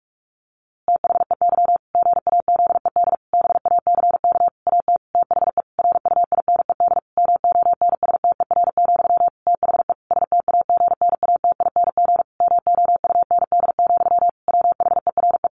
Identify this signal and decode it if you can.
Morse code, heard as a keyed tone, about 44 wpm, 700 Hz. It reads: THEY GAZED BACK AT THE RUINED MONSTER, THE STAGNATING MOUND, WHERE
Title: A Sound Of Thunder 45wpm